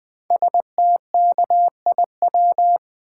Morse code, heard as a keyed tone, about 21 wpm, 700 Hz. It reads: STKIW
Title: Code Groups - 5 Letters - 20wpm